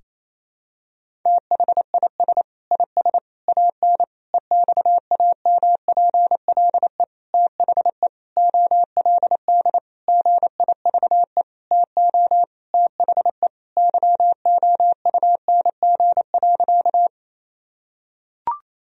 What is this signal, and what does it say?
Morse code, heard as a keyed tone, about 28 wpm, 700 Hz. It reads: THIS IS AN EXAMPLE THE OLD GIVE TO THE YOUNG.
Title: Sentences from Top 1000 Words - Review Part 1 - 28wpm